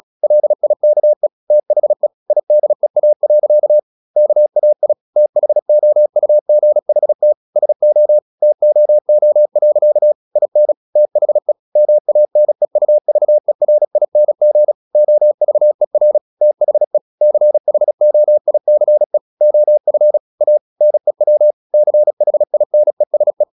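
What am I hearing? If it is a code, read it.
Morse code, heard as a keyed tone, about 36 wpm, 600 Hz. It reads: LIKE THE IDEA. KAI THOUGHT SO TOO. IN THE MANEUVERING OVER THE CHOICE OF A NEW CHINESE
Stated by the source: Morse practice at 36 wpm, text of Ken Follett's novel Never, from CW med Gnister.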